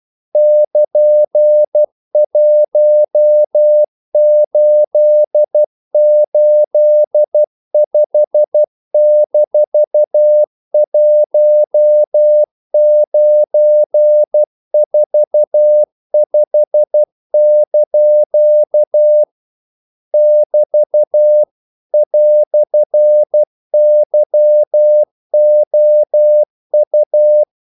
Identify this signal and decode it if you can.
Morse code, heard as a keyed tone, about 12 wpm, 600 Hz. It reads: <KN>1885-1945) <BT> "YOU
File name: CW / Practice / EnglishQuotations0112